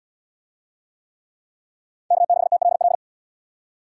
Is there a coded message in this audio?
Yes: Morse code from 2.1 to 3.0 s, heard as a keyed tone, about 68 wpm, 700 Hz. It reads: K6TLL